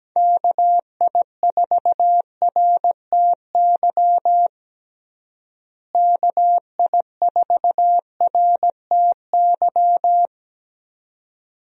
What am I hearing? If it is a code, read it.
Morse code, heard as a keyed tone, about 17 wpm, 700 Hz. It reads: KI4RTY KI4RTY
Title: USA Call Signs - 6 Characters - Repeated 2x - 17wpm